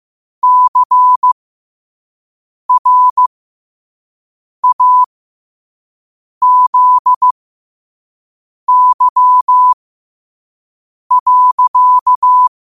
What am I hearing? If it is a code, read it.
Morse code, heard as a keyed tone, about 15 wpm, spaced as for 5 wpm, 1000 Hz. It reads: CRAZY.
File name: qotdfiles / 5wpm